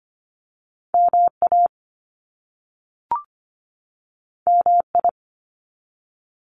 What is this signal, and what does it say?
Morse code, heard as a keyed tone, about 25 wpm, 700 Hz. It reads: MA MI
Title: US States 25wpm